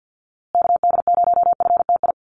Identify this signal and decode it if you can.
Morse code, heard as a keyed tone, about 50 wpm, 700 Hz. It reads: KD0FTI
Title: USA Call Signs Repeated 3x 50wpm